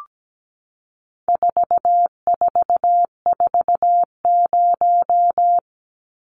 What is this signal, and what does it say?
Morse code, heard as a keyed tone, about 17 wpm, 700 Hz. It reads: 4440
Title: Numbers - 4-Digit 17wpm